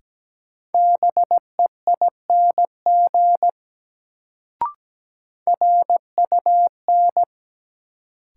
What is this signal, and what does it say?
Morse code, heard as a keyed tone, about 17 wpm, 700 Hz. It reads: BEING RUN
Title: Sets of 4 Words - New Word Drill - 17wpm